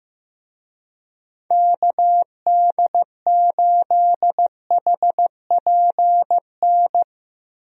Morse code, heard as a keyed tone, about 15 wpm, 700 Hz. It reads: KD8HPN
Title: USA Call Signs - 6 Characters - Repeated 3x - 15wpm